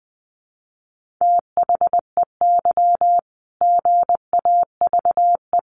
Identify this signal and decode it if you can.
Morse code, heard as a keyed tone, about 19 wpm, 700 Hz. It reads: THEY GAVE
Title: Sentences from Top 1400 Words 20wpm